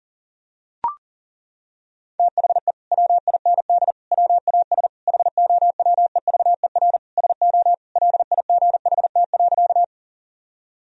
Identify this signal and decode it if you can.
Morse code, heard as a keyed tone, about 40 wpm, 700 Hz. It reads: THE WIND WAS HOWEVER SO LIGHT.
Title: Sentences from Top 1000 Words - Review Part 2 - 40wpm